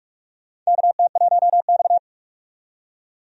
Morse code, heard as a keyed tone, about 46 wpm, 700 Hz. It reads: KT1X
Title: USA Call Signs - 4 Characters - 45wpm